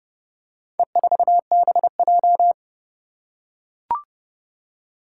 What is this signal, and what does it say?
Morse code, heard as a keyed tone, about 30 wpm, 700 Hz. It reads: E4BJ
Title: Four Challenging Letter-Number Combinations 30wpm